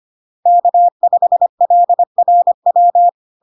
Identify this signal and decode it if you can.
Morse code, heard as a keyed tone, about 25 wpm, 700 Hz. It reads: K5LRW
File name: Log-in-CW-25wpm-700-Hz